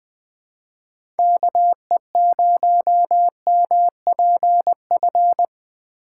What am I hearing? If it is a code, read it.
Morse code, heard as a keyed tone, about 20 wpm, 700 Hz. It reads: KE0MPF
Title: USA Call Signs - 6 Characters - 6x Character Spacing - 20wpm